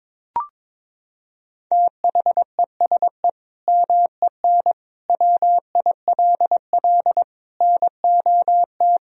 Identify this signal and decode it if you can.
Morse code, heard as a keyed tone, about 22 wpm, 700 Hz. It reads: THESE MEN WILL NOT
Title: Sentences from Top 1000 Words - Review Part 1 - 22wpm